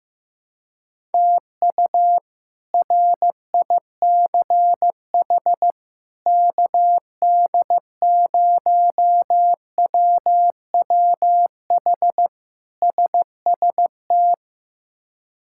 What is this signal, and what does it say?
Morse code, heard as a keyed tone, about 15 wpm, 700 Hz. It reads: TU RICH KD0WWH SST
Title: K1USN SST QSO Practice - 15wpm